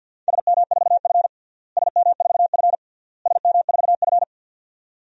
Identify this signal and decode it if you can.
Morse code, heard as a keyed tone, about 47 wpm, 700 Hz. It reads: SM4F SM4F SM4F
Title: DXCC-100 Call Signs - 4 Characters - Repeated 3x - 50wpm